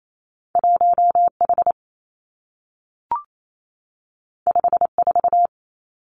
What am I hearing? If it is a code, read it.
Morse code, heard as a keyed tone, about 28 wpm, 700 Hz. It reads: 1 H 5 4
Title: Two Challenging Letter-Number Combinations 28wpm